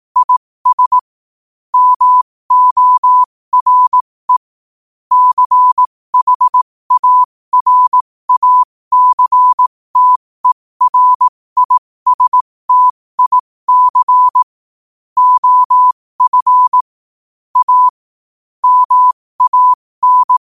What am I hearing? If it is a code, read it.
Morse code, heard as a keyed tone, about 18 wpm, spaced as for 15 wpm, 1000 Hz. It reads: IS MORE CHARACTERISTIC OF A MAN
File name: qotdfiles / 15wpm